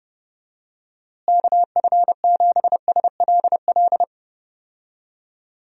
Morse code, heard as a keyed tone, about 30 wpm, 700 Hz. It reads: KF7SLL
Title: USA Call Signs - 6 Characters - Repeated 2x - 30wpm